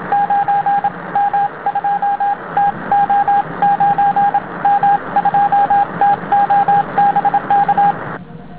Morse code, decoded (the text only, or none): 9M2TO 9M2TO BK